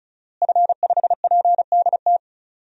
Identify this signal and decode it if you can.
Morse code, heard as a keyed tone, about 34 wpm, 700 Hz. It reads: F5PDT
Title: DXCC-100 Call Signs - 5 Characters - Repeated 2x - 35wpm